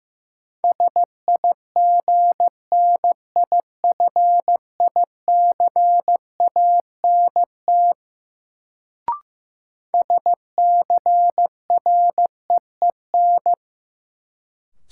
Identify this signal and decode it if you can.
Morse code, heard as a keyed tone, about 15 wpm, 700 Hz. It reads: SIGNIFICANT SCREEN
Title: Sentences from Top 1000 Words - Review Part 6 - New Word Drill - 15wpm